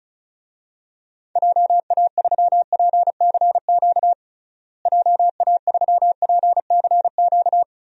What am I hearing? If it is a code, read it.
Morse code, heard as a keyed tone, about 34 wpm, 700 Hz. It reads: JA3PCQ JA3PCQ
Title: DXCC-100 Call Signs - 6 Characters - Repeated 3x - 35wpm